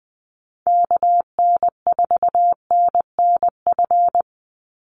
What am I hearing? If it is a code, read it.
Morse code, heard as a keyed tone, about 20 wpm, 700 Hz. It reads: KN4NNF
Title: USA Call Signs 20wpm